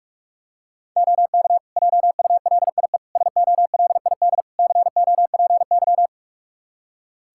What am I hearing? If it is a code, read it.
Morse code, heard as a keyed tone, about 46 wpm, 700 Hz. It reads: OK JULIE SOLID COPY
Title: Practice QSO - Exchange 3 - More Info - 45wpm